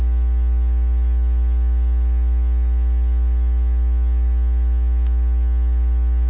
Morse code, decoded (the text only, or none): none